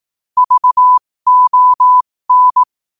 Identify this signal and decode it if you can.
Morse code, heard as a keyed tone, about 18 wpm, spaced as for 15 wpm, 1000 Hz. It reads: VON
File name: qotdfiles / 15wpm